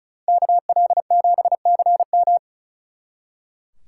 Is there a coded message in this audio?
Yes: Morse code from 0.3 to 2.4 s, heard as a keyed tone, about 46 wpm, 700 Hz. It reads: KL7CM